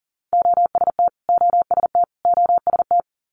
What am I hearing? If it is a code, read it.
Morse code, heard as a keyed tone, about 40 wpm, 700 Hz. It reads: OST OST OST
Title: Multi-letter Phonemes Repeated 3x 40wpm